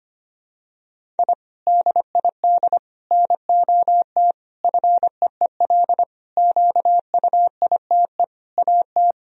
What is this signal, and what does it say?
Morse code, heard as a keyed tone, about 25 wpm, 700 Hz. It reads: I DID NOT FEEL QUITE AT
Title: Sentences from Top 1000 Words - Review Part 1 - 25wpm